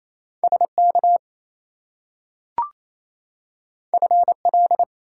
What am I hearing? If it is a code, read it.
Morse code, heard as a keyed tone, about 28 wpm, 700 Hz. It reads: SK FL
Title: US States and Canadian Provinces 28wpm